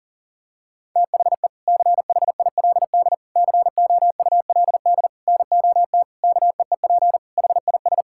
Morse code, heard as a keyed tone, about 37 wpm, 700 Hz. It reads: THE CHILD COULD NOT KEEP HIS
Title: Sentences from Top 1000 Words - Review Part 6 - 40wpm